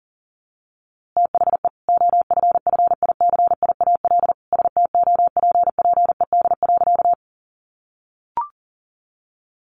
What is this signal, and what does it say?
Morse code, heard as a keyed tone, about 40 wpm, 700 Hz. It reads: THE OFFICIAL STOPPED.
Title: A Sound Of Thunder 40wpm